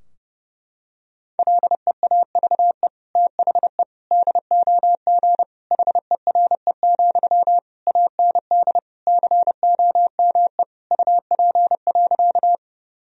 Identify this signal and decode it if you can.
Morse code, heard as a keyed tone, about 32 wpm, 700 Hz. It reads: LEAVE THE DOG HERE, AND COME UP.
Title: Sentences from Top 1000 Words - Review Part 2 - 30wpm